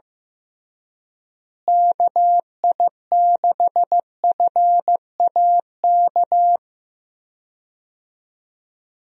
Morse code, heard as a keyed tone, about 15 wpm, 700 Hz. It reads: KI6FAK
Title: USA Call Signs - 6 Characters - Repeated 3x - 15wpm